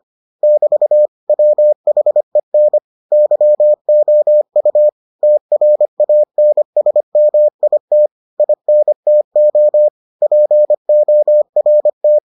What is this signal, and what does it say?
Morse code, heard as a keyed tone, about 25 wpm, 600 Hz. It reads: <BT> WHEN YOU TRANSMIT INTO PORT